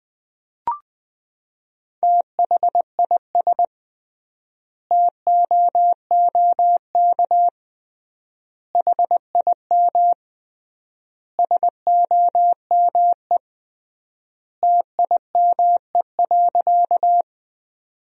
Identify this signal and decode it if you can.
Morse code, heard as a keyed tone, about 21 wpm, 700 Hz. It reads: THIS TOOK HIM SOME TIME.
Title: 3x Word-Spacing - Sentences from Top 300 Words - 20wpm